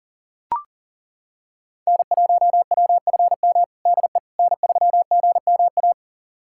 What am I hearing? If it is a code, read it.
Morse code, heard as a keyed tone, about 37 wpm, 700 Hz. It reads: N1WFM DE N3GMA
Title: USA Callsign DE Callsign - 5 Characters - 40wpm